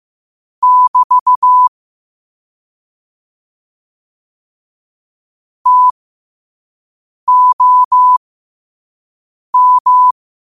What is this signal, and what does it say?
Morse code, heard as a keyed tone, about 15 wpm, spaced as for 5 wpm, 1000 Hz. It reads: <BT> TOM
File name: qotdfiles / 5wpm